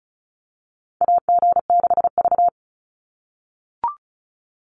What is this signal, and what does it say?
Morse code, heard as a keyed tone, about 36 wpm, 700 Hz. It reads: AG6V
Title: Call Signs - 35wpm